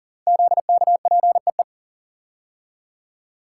Morse code, heard as a keyed tone, about 40 wpm, 700 Hz. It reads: ZKPEE
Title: Code Groups - 5 Letters - 40wpm